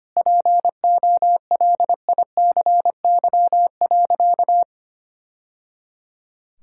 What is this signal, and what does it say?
Morse code, heard as a keyed tone, about 25 wpm, 700 Hz. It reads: POLICY.
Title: Sentences from Top 1000 Words - Review Part 5 - 25wpm